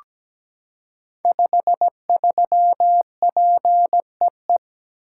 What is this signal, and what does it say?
Morse code, heard as a keyed tone, about 17 wpm, 700 Hz. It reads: 53PEE
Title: Code Groups - 5 Letter:Numbers - 17wpm